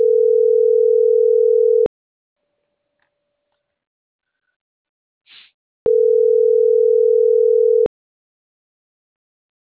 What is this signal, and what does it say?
Ringback tone (North America), 440 + 480 Hz.